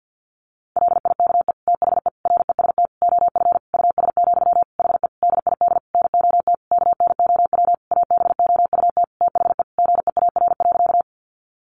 Morse code, heard as a keyed tone, about 50 wpm, 700 Hz. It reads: LIKE THE REST OF US, HE DID NOT KNOW ABOUT THE GEAR.
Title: Sentences from Top 1400 Words 50wpm